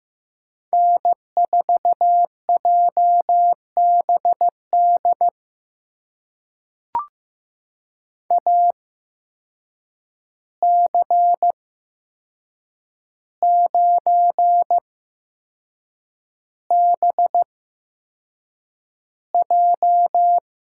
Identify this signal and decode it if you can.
Morse code, heard as a keyed tone, about 15 wpm, spaced as for 6 wpm, 700 Hz. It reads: N4JBD AC9BJ
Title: USA Call Signs - 5 Characters - 8x Character Spacing - 15wpm